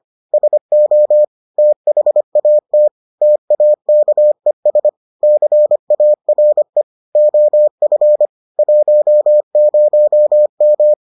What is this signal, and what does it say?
Morse code, heard as a keyed tone, about 25 wpm, 600 Hz. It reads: SO THAT TAKES CARE OF 10M